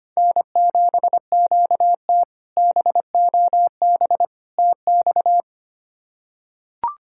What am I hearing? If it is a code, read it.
Morse code, heard as a keyed tone, about 25 wpm, 700 Hz. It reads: N7QT BOB TX
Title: K1USN SST QSO Practice - 25wpm